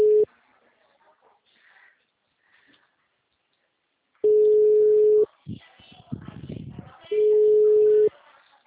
Ringback tone (Europe, 425 Hz).